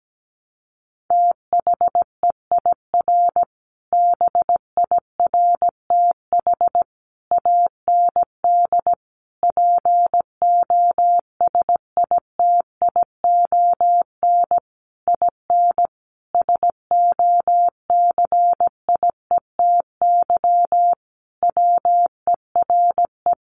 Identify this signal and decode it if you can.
Morse code, heard as a keyed tone, about 17 wpm, 700 Hz. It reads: THEIR BIRTH AND POSITION IN SOCIETY WERE
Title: Sentences from Top 900 Words 17wpm